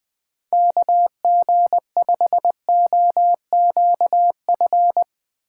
Morse code, heard as a keyed tone, about 20 wpm, 700 Hz. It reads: KG5OQF
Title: USA Call Signs - 6 Characters - 3x Character Spacing - 20wpm